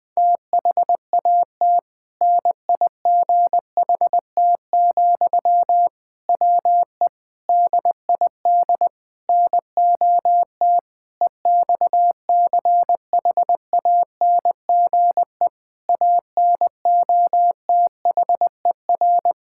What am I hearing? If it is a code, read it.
Morse code, heard as a keyed tone, about 20 wpm, 700 Hz. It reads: THAT NIGHT, WE DID NOT EXCHANGE ANOTHER
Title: Sentences from Top 1000 Words - Review Part 6 - 20wpm